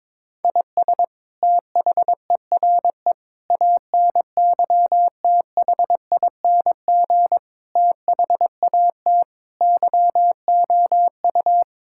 Morse code, heard as a keyed tone, about 22 wpm, 700 Hz. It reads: IS THERE ANYTHING THAT YOU
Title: Sentences from Top 1000 Words - Review Part 6 - 22wpm